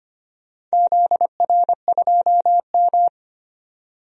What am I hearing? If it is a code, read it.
Morse code, heard as a keyed tone, about 25 wpm, 700 Hz. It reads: ZR2M
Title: DXCC-100 Call Signs - 4 Characters - 25wpm